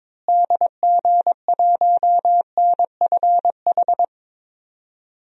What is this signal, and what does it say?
Morse code, heard as a keyed tone, about 22 wpm, 700 Hz. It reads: DG1NFH
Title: DXCC-100 Call Signs - 6 Characters - Repeated 3x - 22wpm